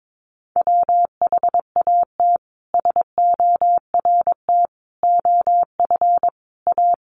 Morse code, heard as a keyed tone, about 22 wpm, 700 Hz. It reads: WHAT SORT OF A
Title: Sentences from Top 1300 Words 22wpm